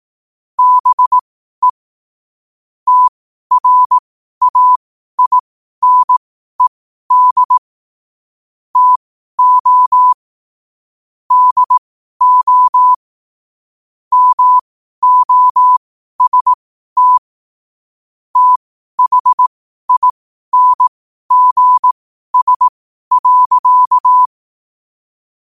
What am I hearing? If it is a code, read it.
Morse code, heard as a keyed tone, about 18 wpm, spaced as for 12 wpm, 1000 Hz. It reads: BE TRAINED TO DO MOST THINGS.
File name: qotdfiles / 12wpm